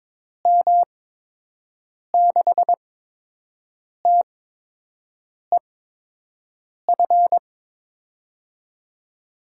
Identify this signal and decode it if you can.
Morse code, heard as a keyed tone, about 22 wpm, spaced as for 6 wpm, 700 Hz. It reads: M6TEF
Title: USA Call Signs - 6 Characters - 8x Character Spacing - 22wpm